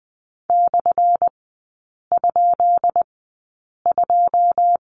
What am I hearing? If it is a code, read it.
Morse code, heard as a keyed tone, about 20 wpm, spaced as for 8 wpm, 700 Hz. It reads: /?2
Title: 178 Lesson 29e - Slash - Three Characters 20wpm